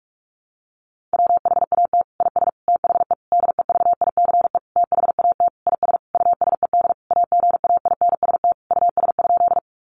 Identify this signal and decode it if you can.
Morse code, heard as a keyed tone, about 44 wpm, 700 Hz. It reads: WHAT IS THE DEVICE THAT IS USED AGAINST US?
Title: Sentences from Top 1500 Words 45wpm